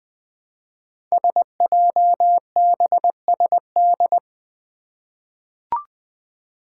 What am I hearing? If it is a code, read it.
Morse code, heard as a keyed tone, about 20 wpm, 700 Hz. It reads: SJBSD
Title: Five Challenging Letter-Number Combinations 20wpm